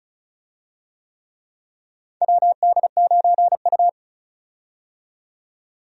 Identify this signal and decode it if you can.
Morse code, heard as a keyed tone, about 34 wpm, 700 Hz. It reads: WD9U
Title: USA Call Signs - 4 Characters - 35wpm